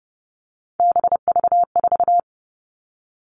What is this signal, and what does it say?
Morse code, heard as a keyed tone, about 30 wpm, 700 Hz. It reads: BV4
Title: All Cut Numbers - 3-Digit 30wpm